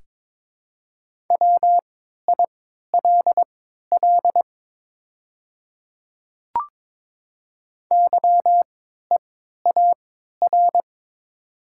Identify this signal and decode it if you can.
Morse code, heard as a keyed tone, about 22 wpm, spaced as for 12 wpm, 690 Hz. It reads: WILL YEAR
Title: 3x Character Spacing - Top 300 Words - 22wpm